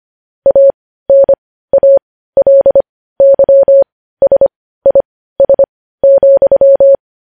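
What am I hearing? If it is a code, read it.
Morse code, heard as a keyed tone, about 25 wpm, spaced as for 15 wpm, 550 Hz. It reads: ANALYSIS,
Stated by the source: Morse code news headlines sent at 15.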